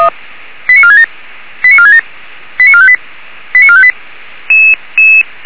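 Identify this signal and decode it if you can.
Five-tone selective call (ZVEI), heard as quick runs of five tones, tones of about 70 ms. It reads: 89467 89467 89458 89458
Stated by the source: ZVEI five-tone selective call received by SDR.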